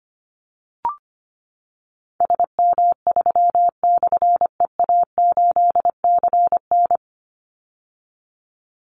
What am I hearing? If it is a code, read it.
Morse code, heard as a keyed tone, about 25 wpm, 700 Hz. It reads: SM3/EA8CN
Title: Call Signs - 25wpm